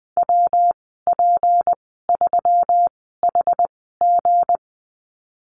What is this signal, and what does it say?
Morse code, heard as a keyed tone, about 20 wpm, spaced as for 15 wpm, 700 Hz. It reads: WP3HG
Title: 161 Lesson 26g - Letter g - Call Signs 20wpm